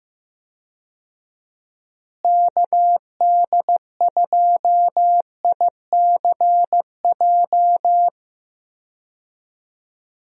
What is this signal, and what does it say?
Morse code, heard as a keyed tone, about 15 wpm, 700 Hz. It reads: KD2ICJ
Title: USA Call Signs - 6 Characters - Repeated 3x - 15wpm